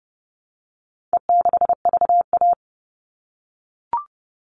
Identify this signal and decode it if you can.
Morse code, heard as a keyed tone, about 28 wpm, 700 Hz. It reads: E6VA
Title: All Cut Numbers - 4-Digit 30wpm